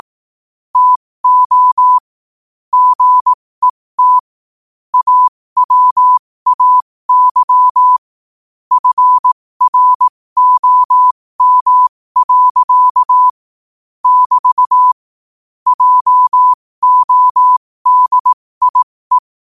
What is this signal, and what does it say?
Morse code, heard as a keyed tone, about 18 wpm, spaced as for 15 wpm, 1000 Hz. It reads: TO GET AWAY FROM. <BT> JODIE
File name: qotdfiles / 15wpm